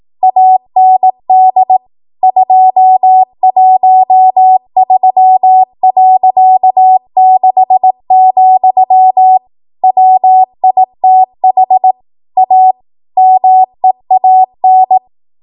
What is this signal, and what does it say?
Morse code, heard as a keyed tone, about 18 wpm, 750 Hz. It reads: AND 213.6, WITH A MEAN
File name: Archive / 18 WPM / 18WPM